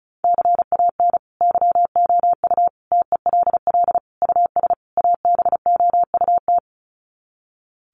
Morse code, heard as a keyed tone, about 34 wpm, 700 Hz. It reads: CAN YOU TELL US ABOUT
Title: Sets of 5 Words Encore2 35wpm